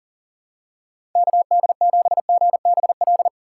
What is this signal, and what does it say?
Morse code, heard as a keyed tone, about 40 wpm, 700 Hz. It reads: KD7GBL
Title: USA Call Signs - 6 Characters - 6x Character Spacing - 40wpm